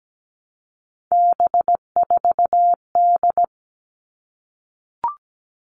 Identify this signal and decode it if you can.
Morse code, heard as a keyed tone, about 17 wpm, 700 Hz. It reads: B4D
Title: All Cut Numbers - 3-Digit 17wpm